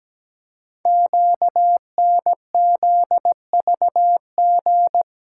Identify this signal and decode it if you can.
Morse code, heard as a keyed tone, about 17 wpm, 700 Hz. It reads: QNZVG
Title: Code Groups - 5 Letters - 17wpm